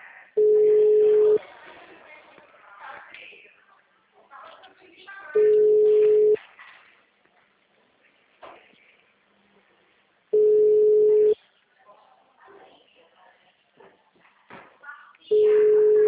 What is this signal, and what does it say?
Ringback tone (Europe, 425 Hz), with speech.
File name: out- -2023- - - .498